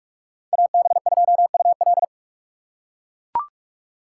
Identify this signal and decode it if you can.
Morse code, heard as a keyed tone, about 40 wpm, 700 Hz. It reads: AB2UL